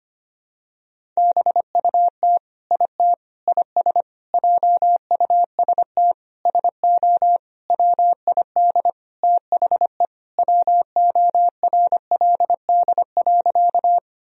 Morse code, heard as a keyed tone, about 25 wpm, 700 Hz. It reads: BUT IT IS JUST SO WID THE WORLD.
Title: Sentences from Top 200 Words and Abbreviations 25wpm